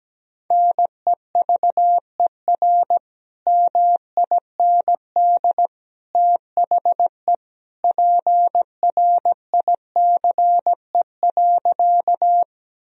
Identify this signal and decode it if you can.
Morse code, heard as a keyed tone, about 17 wpm, 700 Hz. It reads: NEVER MIND THE PRICE.
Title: Sentences from Top 1000 Words - Review Part 4 - 17wpm